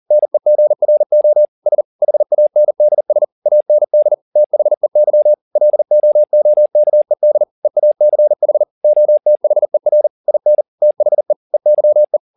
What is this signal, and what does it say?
Morse code, heard as a keyed tone, about 40 wpm, 600 Hz. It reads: NEGRO S HANDS AND THEY LOOKED EACH OTHER IN THE EYE
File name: the old man and the sea 2 40wpm